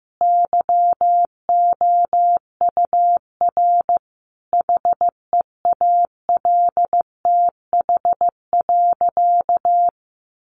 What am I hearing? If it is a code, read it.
Morse code, heard as a keyed tone, about 15 wpm, 700 Hz. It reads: YOUR HEALTH.
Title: Sentences from Top 600 Words 15wpm